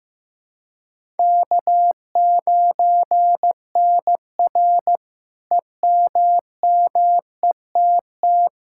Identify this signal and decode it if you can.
Morse code, heard as a keyed tone, about 16 wpm, 700 Hz. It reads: K9NR EMMETT
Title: K1USN SST QSO Practice - 15wpm